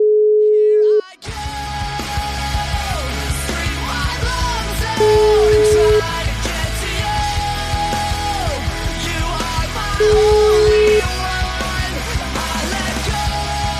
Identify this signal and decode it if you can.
Ringback tone (Europe, 425 Hz), with noise.